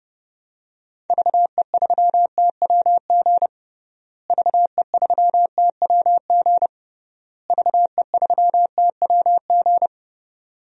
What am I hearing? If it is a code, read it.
Morse code, heard as a keyed tone, about 30 wpm, 700 Hz. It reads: VE3TWG VE3TWG VE3TWG
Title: DXCC-100 Call Signs - 6 Characters - Repeated 3x - 30wpm